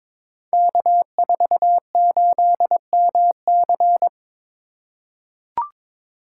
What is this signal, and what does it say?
Morse code, heard as a keyed tone, about 22 wpm, 700 Hz. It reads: K48MC
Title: Code Groups - 5 Letter:Numbers - 22wpm